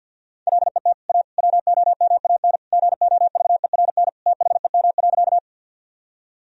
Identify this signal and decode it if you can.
Morse code, heard as a keyed tone, about 53 wpm, 700 Hz. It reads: LET A WOMAN GOVERN THEM.
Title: Sentences from Top 1000 Words - Review Part 2 - 50wpm